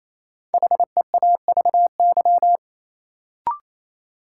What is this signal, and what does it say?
Morse code, heard as a keyed tone, about 28 wpm, 700 Hz. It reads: HEAVY
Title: Sentences from Top 1000 Words - Review Part 6 - New Word Drill - 28wpm